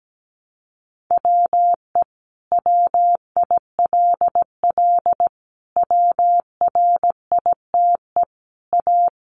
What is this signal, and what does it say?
Morse code, heard as a keyed tone, about 17 wpm, 700 Hz. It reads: WE WILL WRITE A
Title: Sentences from Top 1300 Words 17wpm